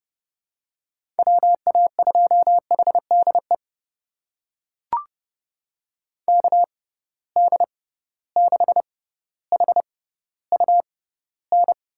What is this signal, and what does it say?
Morse code, heard as a keyed tone, about 30 wpm, spaced as for 15 wpm, 700 Hz. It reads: WA2HDE KD6HUN
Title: USA Call Signs - 6 Characters - 6x Character Spacing - 30wpm